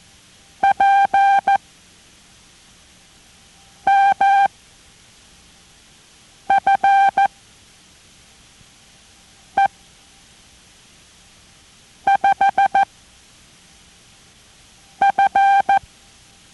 Morse code, heard as a keyed tone, about 14 wpm, spaced as for 4 wpm, 790 Hz. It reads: PMFE5F